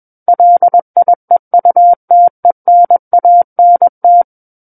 Morse code, heard as a keyed tone, about 21 wpm, 700 Hz. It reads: LIEUTENANT
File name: Book One / 12-Chapter 12 - 21 WPM